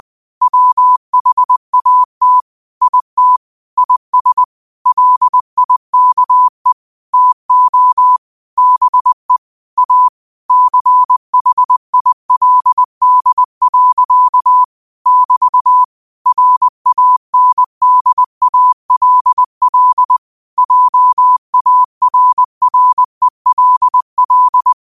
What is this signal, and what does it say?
Morse code, heard as a keyed tone, about 20 wpm, 1000 Hz. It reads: WHAT IT IS LIKE TO BE A CHILD. <BT> RANDALL JARRELL
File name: qotdfiles / 20wpm